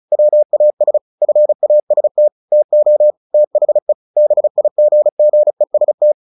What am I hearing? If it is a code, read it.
Morse code, heard as a keyed tone, about 34 wpm, 600 Hz. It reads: WAS FAST TO THE BIGGEST
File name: the old man and the sea / the old man and the sea 1 35wpm